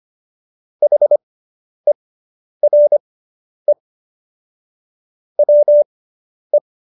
Morse code, heard as a keyed tone, about 25 wpm, spaced as for 10 wpm, 590 Hz. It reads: HERE WE